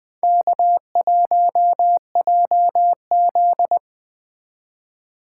Morse code, heard as a keyed tone, about 21 wpm, 700 Hz. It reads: K1JZ
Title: Speed Racing - USA 1x2 Call Signs 17wpm